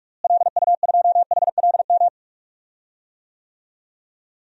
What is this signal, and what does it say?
Morse code, heard as a keyed tone, about 41 wpm, 700 Hz. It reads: LU2HLM
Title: DXCC-100 Call Signs - 6 Characters - Repeated 3x - 45wpm